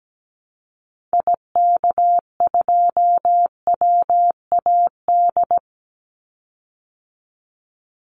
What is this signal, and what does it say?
Morse code, heard as a keyed tone, about 17 wpm, 700 Hz. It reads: IK2WAD
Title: Call Signs Repeated 3x - 17wpm